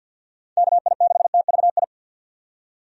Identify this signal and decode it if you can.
Morse code, heard as a keyed tone, about 47 wpm, 700 Hz. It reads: KI6TVI
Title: USA Call Signs - 6 Characters - 50wpm